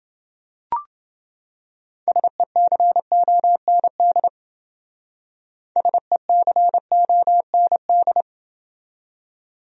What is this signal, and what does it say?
Morse code, heard as a keyed tone, about 29 wpm, 700 Hz. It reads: SECOND SECOND